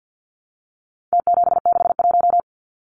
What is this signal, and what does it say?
Morse code, heard as a keyed tone, about 50 wpm, 700 Hz. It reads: T761
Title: Numbers - Contest Style - 4-Digit 50wpm